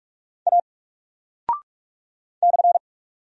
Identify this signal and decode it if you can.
Morse code, heard as a keyed tone, about 44 wpm, 700 Hz. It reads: A /
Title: Single Character (Letter:Number:Punctuation:Prosign) Alt 45wpm